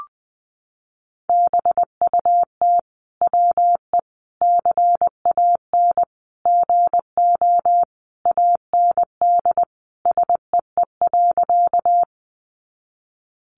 Morse code, heard as a keyed tone, about 20 wpm, 700 Hz. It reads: BUT WE CAN GO AND SEE.
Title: Sentences from Top 100 Words 20wpm